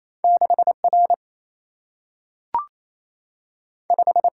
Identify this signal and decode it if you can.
Morse code, heard as a keyed tone, about 28 wpm, 700 Hz. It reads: 6R 5
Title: Single-Letter Single-Number 28wpm